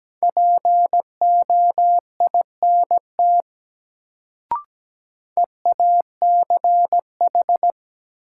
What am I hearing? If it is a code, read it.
Morse code, heard as a keyed tone, about 17 wpm, 700 Hz. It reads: POINT EACH
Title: Sentences from Top 200 Words - New Word Drill - 17wpm